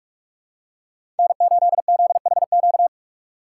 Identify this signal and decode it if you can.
Morse code, heard as a keyed tone, about 44 wpm, 700 Hz. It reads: N8ZHQ
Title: USA Call Signs - 5 Characters - 2x Character Spacing - 45wpm